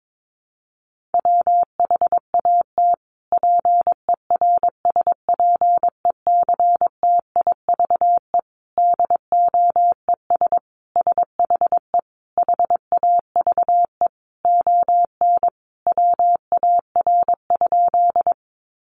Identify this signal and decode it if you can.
Morse code, heard as a keyed tone, about 22 wpm, 700 Hz. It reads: WHAT PERSPECTIVE DOES SHE HAVE ON WAR?
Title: Sentences from Top 1200 Words 22wpm